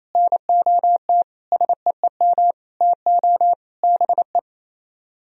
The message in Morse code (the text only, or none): NOT SEEM TO BE